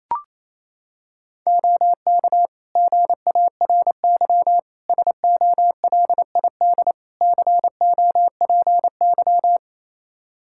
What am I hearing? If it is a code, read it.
Morse code, heard as a keyed tone, about 28 wpm, 700 Hz. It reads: OK GARY SOLID COPY
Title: Practice QSO - Exchange 3 - More Info - 28wpm